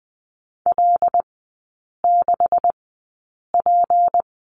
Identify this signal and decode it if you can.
Morse code, heard as a keyed tone, about 20 wpm, spaced as for 8 wpm, 700 Hz. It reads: L6P
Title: 172 Lesson 28e - Number 9 - Three Characters 20wpm